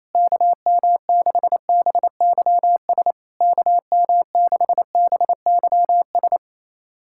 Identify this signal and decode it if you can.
Morse code, heard as a keyed tone, about 28 wpm, 690 Hz. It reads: KM6BYS KM6BYS
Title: SOTA QSO Practice 28wpm